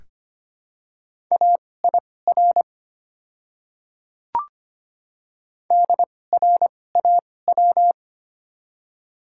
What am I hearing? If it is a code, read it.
Morse code, heard as a keyed tone, about 25 wpm, spaced as for 18 wpm, 700 Hz. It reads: AIR DRAW
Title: 2x Character Spacing - Top 300 Words - 25wpm